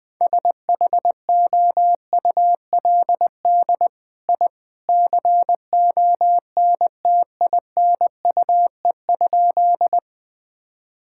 Morse code, heard as a keyed tone, about 20 wpm, 700 Hz. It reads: SHOULD I CONTINUE?
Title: Sentences from Top 1000 Words - Review Part 6 - 20wpm